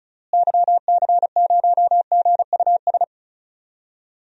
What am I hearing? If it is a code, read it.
Morse code, heard as a keyed tone, about 36 wpm, 700 Hz. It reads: YC0GUS
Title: DXCC-100 Call Signs - 6 Characters - Repeated 3x - 35wpm